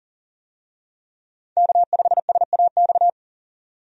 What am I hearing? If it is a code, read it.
Morse code, heard as a keyed tone, about 40 wpm, 700 Hz. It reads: K5SAX
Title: USA Call Signs - 5 Characters - 40wpm